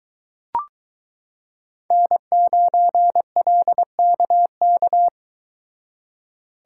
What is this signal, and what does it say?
Morse code, heard as a keyed tone, about 22 wpm, 700 Hz. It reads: N9LKK